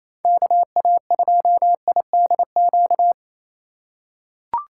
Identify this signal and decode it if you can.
Morse code, heard as a keyed tone, about 27 wpm, 700 Hz. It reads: KA2IDQ
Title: USA Call Signs - 6 Characters - 2x Character Spacing - 28wpm